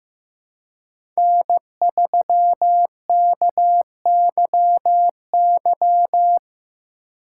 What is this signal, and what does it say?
Morse code, heard as a keyed tone, about 15 wpm, 700 Hz. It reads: N3KYY
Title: USA Call Signs - 5 Characters - 2x Character Spacing - 15wpm